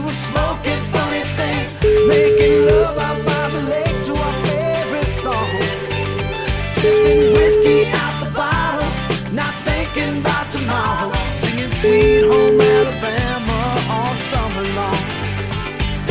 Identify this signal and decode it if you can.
Ringback tone (Europe, 425 Hz), with noise.